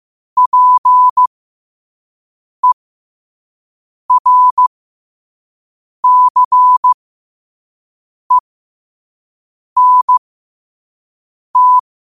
Morse code, heard as a keyed tone, about 15 wpm, spaced as for 5 wpm, 1000 Hz. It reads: PERCENT
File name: qotdfiles / 5wpm